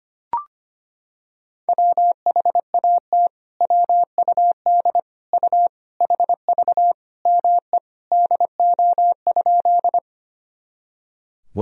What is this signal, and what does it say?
Morse code, heard as a keyed tone, about 25 wpm, 700 Hz. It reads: WHAT WUD U HV ME DO?
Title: Sentences from Top 200 Words and Abbreviations 25wpm